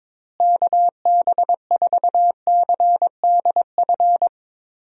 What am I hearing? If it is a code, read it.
Morse code, heard as a keyed tone, about 22 wpm, 700 Hz. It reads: KB4CDF
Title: USA Call Signs - 6 Characters - Repeated 2x - 22wpm